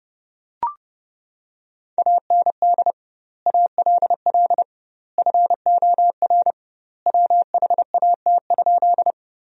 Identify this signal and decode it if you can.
Morse code, heard as a keyed tone, about 29 wpm, 700 Hz. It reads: AND ALL FOR WHAT?
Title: 2x Word-Spacing - Sentences from Top 100 Words - 30wpm